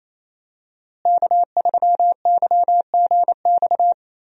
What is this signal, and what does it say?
Morse code, heard as a keyed tone, about 28 wpm, 700 Hz. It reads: K3YGX
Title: USA Call Signs - 5 Characters - Repeated 2x - 28wpm